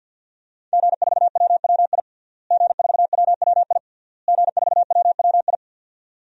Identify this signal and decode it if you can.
Morse code, heard as a keyed tone, about 52 wpm, 700 Hz. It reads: G4WWI G4WWI G4WWI
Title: DXCC-100 Call Signs - 5 Characters - Repeated 3x - 50wpm